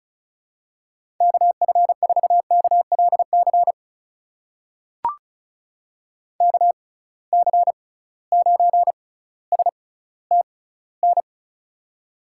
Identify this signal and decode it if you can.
Morse code, heard as a keyed tone, about 36 wpm, spaced as for 18 wpm, 700 Hz. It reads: KF4KLC KC9STN
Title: USA Call Signs - 6 Characters - 6x Character Spacing - 35wpm